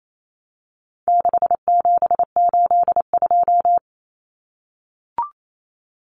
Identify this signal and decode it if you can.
Morse code, heard as a keyed tone, about 28 wpm, 700 Hz. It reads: 6782
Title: Common Cut Numbers - 4-Digit 28wpm